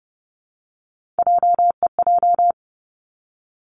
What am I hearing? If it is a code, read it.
Morse code, heard as a keyed tone, about 30 wpm, 700 Hz. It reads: JEJ
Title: Three Challenging Letter-Number Combinations 30wpm